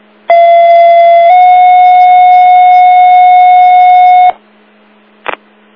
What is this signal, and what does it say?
Two-tone page, heard as a one-second tone, then a three-second tone, about 680 then 730 Hz. It is heard from 0.3 to 4.3 s.